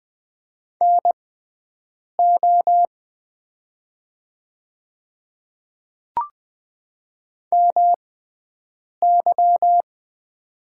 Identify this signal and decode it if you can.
Morse code, heard as a keyed tone, about 20 wpm, spaced as for 7 wpm, 700 Hz. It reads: NO MY
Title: 6x Character Spacing - Top 100 Words - 20wpm